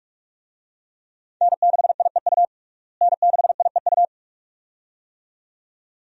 Morse code, heard as a keyed tone, about 42 wpm, 700 Hz. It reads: N6IEU N6IEU
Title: USA Call Signs - 5 Characters - Repeated 2x - 45wpm